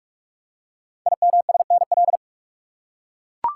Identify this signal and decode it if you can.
Morse code, heard as a keyed tone, about 44 wpm, 700 Hz. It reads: IMSNL
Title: Code Groups - 5 Letters - 45wpm